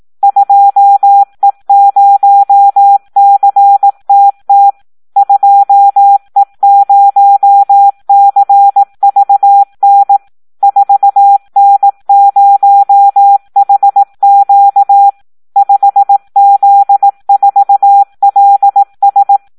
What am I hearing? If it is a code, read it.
Morse code, heard as a keyed tone, about 18 wpm, 800 Hz. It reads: 2E0CTT 2E0CVN 4N0HQ 5Z4LS
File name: Audio practica CW / SD 1 18